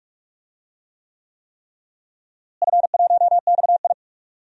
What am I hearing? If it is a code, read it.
Morse code, heard as a keyed tone, about 46 wpm, 700 Hz. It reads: F1XI
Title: DXCC-100 Call Signs - 4 Characters - Repeated 3x - 45wpm